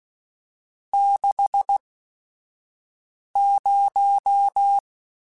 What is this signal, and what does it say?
Morse code, heard as a keyed tone, about 16 wpm, spaced as for 5 wpm, 800 Hz. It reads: 60
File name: Technical Subjects / Technical Subjects-mp3s / Sw Pwr Sup @5